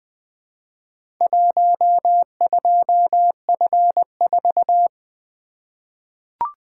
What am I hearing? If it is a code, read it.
Morse code, heard as a keyed tone, about 20 wpm, 700 Hz. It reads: 12F4
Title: Four Letter-Number Combinations - 20wpm